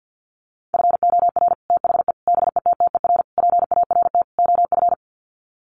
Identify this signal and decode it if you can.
Morse code, heard as a keyed tone, about 50 wpm, 700 Hz. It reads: FOR THE BETTER PART OF
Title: Sets of 5 Words Encore2 50wpm